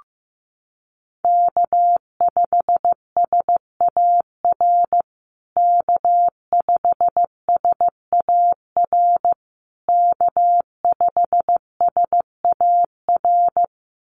Morse code, heard as a keyed tone, about 15 wpm, 700 Hz. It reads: K5SAR K5SAR K5SAR
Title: USA Call Signs Repeated 3x 15wpm